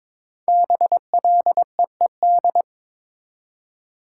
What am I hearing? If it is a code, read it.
Morse code, heard as a keyed tone, about 22 wpm, 700 Hz. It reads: BLEED
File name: 5 Letter Words - First 3 Letters Given - 22wpm